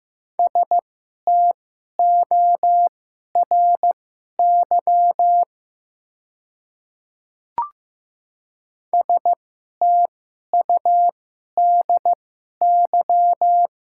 Morse code, heard as a keyed tone, about 15 wpm, spaced as for 11 wpm, 700 Hz. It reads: STORY STUDY
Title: 2x Character Spacing - Top 300 Words - 15wpm